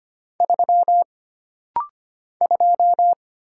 Morse code, heard as a keyed tone, about 25 wpm, 700 Hz. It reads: 3 2
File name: Single Character (Letter:Number:Punctuation:Prosign) - Rapid-Fire - 25wpm